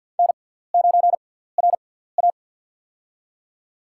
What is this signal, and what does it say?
Morse code, heard as a keyed tone, about 47 wpm, spaced as for 17 wpm, 700 Hz. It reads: N9RA
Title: USA Call Signs - 4 Characters - 6x Character Spacing - 50wpm